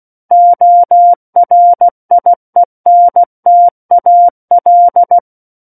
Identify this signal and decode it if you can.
Morse code, heard as a keyed tone, about 16 wpm, 700 Hz. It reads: ORIENTAL
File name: Book One / 07-Chapter 7 - 16 WPM